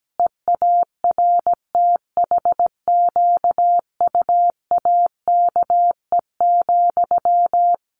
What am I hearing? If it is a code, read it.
Morse code, heard as a keyed tone, about 17 wpm, 700 Hz. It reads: EARTHQUAKE,
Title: A Sound Of Thunder 17wpm